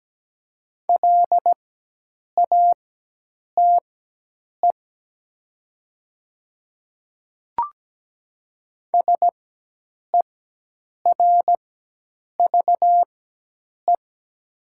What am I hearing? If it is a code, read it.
Morse code, heard as a keyed tone, about 17 wpm, spaced as for 8 wpm, 700 Hz. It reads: LATE SERVE
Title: 4x Character Spacing - Top 500 Words - 17wpm